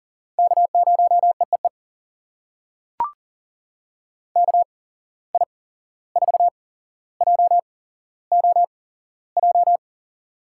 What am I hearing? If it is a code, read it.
Morse code, heard as a keyed tone, about 39 wpm, spaced as for 16 wpm, 700 Hz. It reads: K0EEE KI4JOJ
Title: USA Call Signs - 8x Character Spacing - 40wpm